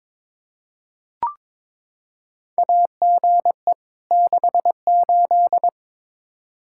Morse code, heard as a keyed tone, about 22 wpm, 700 Hz. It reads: AGE 68
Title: Practice QSO - Exchange 3 - More Info - 22wpm